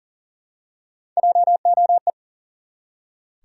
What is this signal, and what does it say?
Morse code, heard as a keyed tone, about 40 wpm, 700 Hz. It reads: JOE
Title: USA Common Names - Max 3 Letters - 40wpm